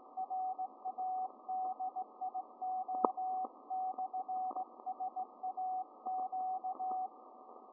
Morse code, heard as a keyed tone, about 17 wpm, 750 Hz, with radio noise. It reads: RADIO/SAQ